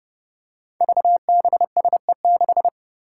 Morse code, heard as a keyed tone, about 32 wpm, 700 Hz. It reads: VBSE6
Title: Five Challenging Letter-Number Combinations 30wpm